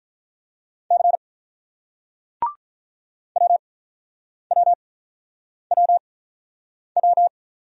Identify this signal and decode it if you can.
Morse code, heard as a keyed tone, about 48 wpm, 700 Hz. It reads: C W W W W